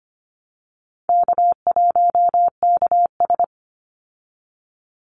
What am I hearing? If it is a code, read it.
Morse code, heard as a keyed tone, about 25 wpm, 700 Hz. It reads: K1KS
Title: Easy Call Signs Repeated 2x - 25wpm